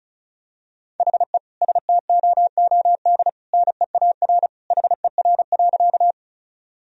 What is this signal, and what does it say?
Morse code, heard as a keyed tone, about 36 wpm, 700 Hz. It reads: HE STOOD NEAR HER.
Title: Sentences from Top 1000 Words - Review Part 2 - 35wpm